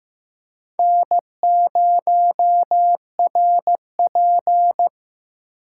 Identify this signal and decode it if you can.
Morse code, heard as a keyed tone, about 15 wpm, 700 Hz. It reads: N0RP
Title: USA Call Signs - 4 Characters - 2x Character Spacing - 15wpm